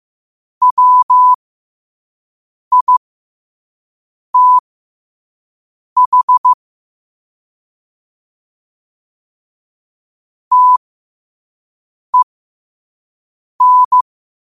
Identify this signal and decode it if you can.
Morse code, heard as a keyed tone, about 15 wpm, spaced as for 5 wpm, 1000 Hz. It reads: WITH TEN
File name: qotdfiles / 5wpm